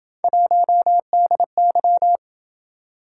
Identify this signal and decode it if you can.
Morse code, heard as a keyed tone, about 27 wpm, 700 Hz. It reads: 1DY